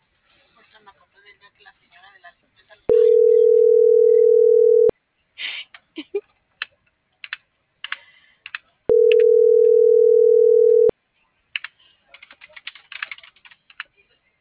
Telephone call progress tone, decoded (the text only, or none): ringback tone